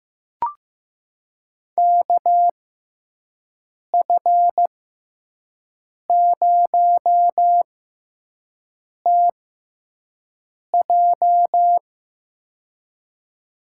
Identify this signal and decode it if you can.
Morse code, heard as a keyed tone, about 15 wpm, spaced as for 5 wpm, 700 Hz. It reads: KF0TJ
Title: USA Call Signs - 5 Characters - 6x Character Spacing - 15wpm